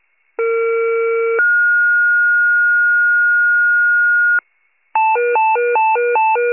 Two-tone page, heard as a one-second tone, then a three-second tone, about 460 then 1430 Hz. It is heard from 0.4 to 4.4 s.
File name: pager / 21 FIRE / wcalert